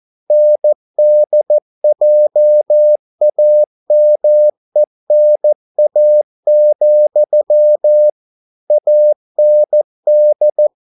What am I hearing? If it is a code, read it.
Morse code, heard as a keyed tone, about 14 wpm, 600 Hz. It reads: NDJAMENA, AND